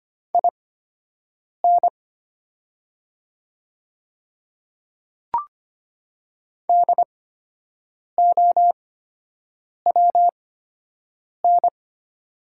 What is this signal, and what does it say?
Morse code, heard as a keyed tone, about 25 wpm, spaced as for 7 wpm, 700 Hz. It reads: IN DOWN
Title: 8x Character Spacing - Top 100 Words - 25wpm